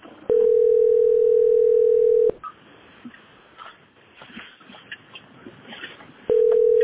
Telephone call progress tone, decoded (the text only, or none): ringback tone